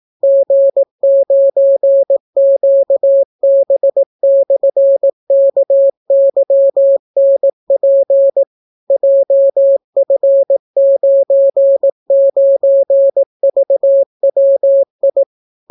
Morse code, heard as a keyed tone, about 18 wpm, 550 Hz. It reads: G9QB/KYNP JF99VWI